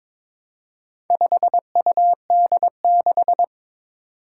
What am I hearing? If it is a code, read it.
Morse code, heard as a keyed tone, about 22 wpm, 700 Hz. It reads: 5UD6
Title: Four Challenging Letter-Number Combinations 22wpm